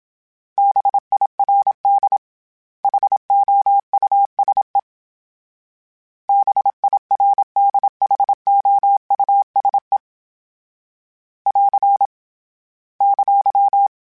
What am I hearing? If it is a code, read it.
Morse code, heard as a keyed tone, about 27 wpm, 800 Hz. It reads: BIRD HOUSE BIRDHOUSE <AR> !
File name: General / General-mp3s / Houses @25